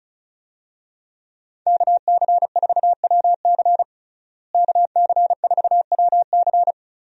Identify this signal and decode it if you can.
Morse code, heard as a keyed tone, about 36 wpm, 700 Hz. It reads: KC4WC KC4WC
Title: USA Call Signs - 5 Characters - Repeated 3x - 35wpm